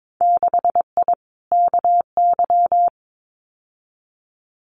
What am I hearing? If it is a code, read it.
Morse code, heard as a keyed tone, about 22 wpm, 700 Hz. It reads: 6I KY
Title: Winter Field Day Exchanges Encore 22wpm